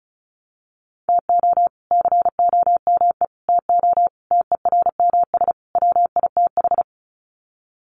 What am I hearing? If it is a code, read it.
Morse code, heard as a keyed tone, about 34 wpm, 700 Hz. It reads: TO COME TO TERMS WITH
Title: Sets of 5 Words Encore 35wpm